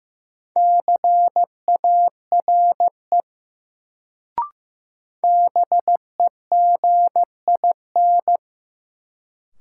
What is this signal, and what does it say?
Morse code, heard as a keyed tone, about 15 wpm, 700 Hz. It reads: CARE BEGIN
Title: Sentences from Top 300 Words - New Word Drill - 15wpm